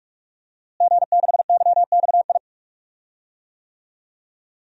Morse code, heard as a keyed tone, about 46 wpm, 700 Hz. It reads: G6YXI
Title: DXCC-100 Call Signs - 5 Characters - Repeated 3x - 45wpm